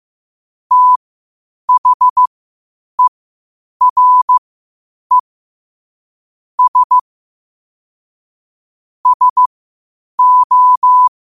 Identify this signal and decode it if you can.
Morse code, heard as a keyed tone, about 15 wpm, spaced as for 8 wpm, 1000 Hz. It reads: THERE S SO
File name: qotdfiles / 8wpm